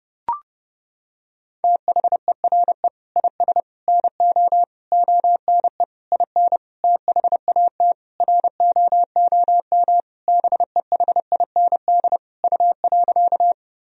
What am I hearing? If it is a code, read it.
Morse code, heard as a keyed tone, about 32 wpm, 700 Hz. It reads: THERE IS NO ONE IN THAT ROOM BEHIND U.
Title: Sentences from Top 500 Words and Abbreviations 30wpm